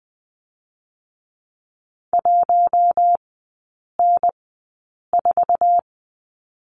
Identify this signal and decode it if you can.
Morse code, heard as a keyed tone, about 20 wpm, spaced as for 8 wpm, 700 Hz. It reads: 1N4
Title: 145 Lesson 24e - Letter y - Three Characters 20wpm